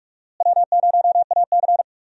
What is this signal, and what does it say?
Morse code, heard as a keyed tone, about 47 wpm, 700 Hz. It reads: W0AC
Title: USA Call Signs - 4 Characters - Repeated 2x - 45wpm